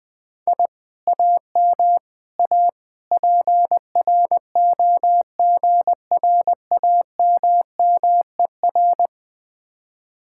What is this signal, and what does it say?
Morse code, heard as a keyed tone, about 20 wpm, 700 Hz. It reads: I AM A PROGRAMMER
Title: Speed-Racing Common QSO Phrases - 20wpm